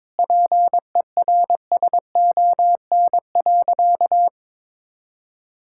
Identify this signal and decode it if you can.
Morse code, heard as a keyed tone, about 22 wpm, 700 Hz. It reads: PERSON.
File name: practice / 22wpm / Sentences with 3-Word Phrases - 22wpm